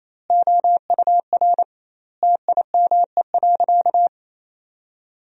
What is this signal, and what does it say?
Morse code, heard as a keyed tone, about 27 wpm, 700 Hz. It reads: OUR TIME.
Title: 2x Word-Spacing - Sentences from Top 200 Words - 28wpm